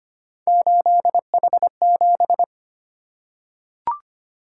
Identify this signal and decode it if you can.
Morse code, heard as a keyed tone, about 25 wpm, 700 Hz. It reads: 8H7
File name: Three Letter-Number Combinations - 25wpm